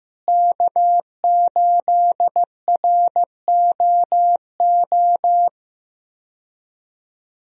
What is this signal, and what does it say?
Morse code, heard as a keyed tone, about 15 wpm, 700 Hz. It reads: K8ROO
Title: USA Call Signs - 5 Characters - Repeated 3x - 15wpm